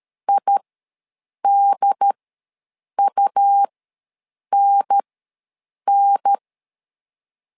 Morse code, heard as a keyed tone, about 13 wpm, spaced as for 7 wpm, 800 Hz. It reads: IDUNN
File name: LEZIONI / 3 MWRO / 3 M ws 12wpm 1745